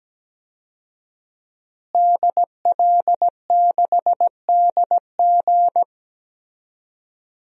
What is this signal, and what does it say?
Morse code, heard as a keyed tone, about 17 wpm, 700 Hz. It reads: DL6DG
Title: DXCC-100 Call Signs - 5 Characters - Repeated 3x - 17wpm